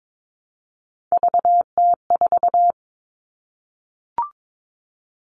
Morse code, heard as a keyed tone, about 22 wpm, 700 Hz. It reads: VT4
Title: All Cut Numbers - 3-Digit 22wpm